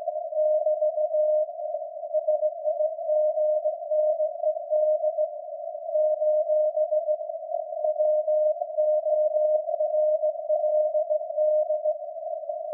Morse code, heard as a keyed tone, about 18 wpm, 610 Hz. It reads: <BT> SIGNED : WORLD
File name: iz7slz / VLF / SAQ jun 2019